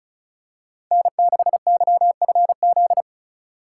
Morse code, heard as a keyed tone, about 36 wpm, 700 Hz. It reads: N6YFZ
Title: USA Call Signs - 5 Characters - 2x Character Spacing - 35wpm